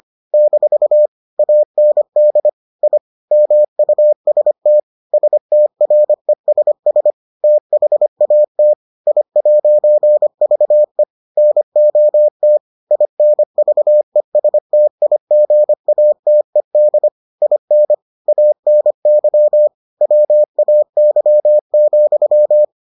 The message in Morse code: - AND I MUST STRESS THAT I'VE NOT INVESTIGATED IN ANY WAY,

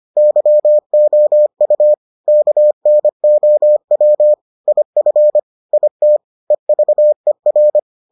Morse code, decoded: YOU KNOW IF IT EVER

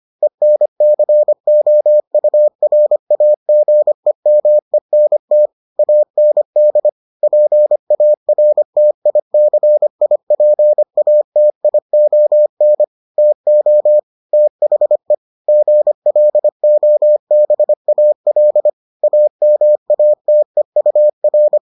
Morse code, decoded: ENCOURAGEMENT AND PARTICIPATION TO THE GLOBAL AMATEUR

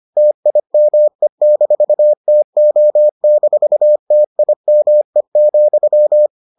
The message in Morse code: TIME-TO-TIME,

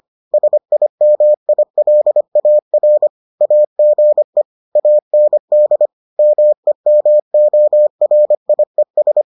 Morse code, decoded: SIMILAR AGE AND MEMORIES

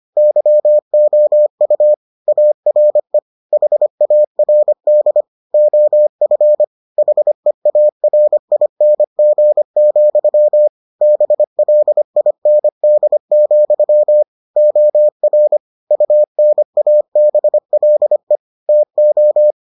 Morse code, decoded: YOU ARE HARD OF HEARING, BLIND, OR UNABLE TO

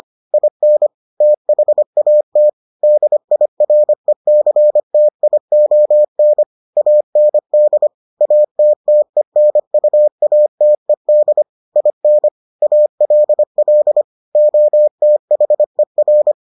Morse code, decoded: IN THAT DIRECTION AND ATTENUATED IN ALL OTHER

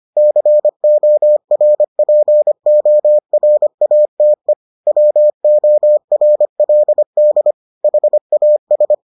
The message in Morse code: CORPORATE WORLD HAS

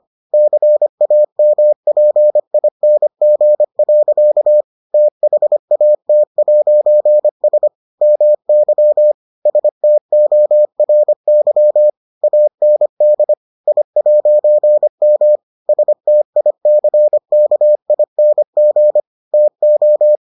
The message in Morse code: CAMPING. THAT'S MY STORY AND I'M STICKING TO

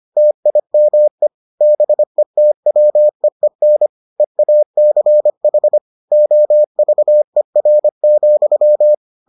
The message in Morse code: TIME BETWEEN EACH OVER,

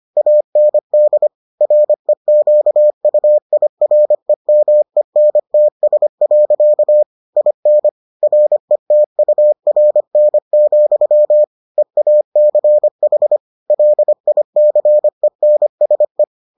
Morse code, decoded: AND REQUIREMENTS. IN RETURN, EACH LICENSE

